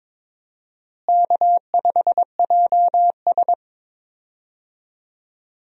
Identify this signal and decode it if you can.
Morse code, heard as a keyed tone, about 22 wpm, 700 Hz. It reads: K5JS
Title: USA Call Signs - 4 Characters - Repeated 3x - 22wpm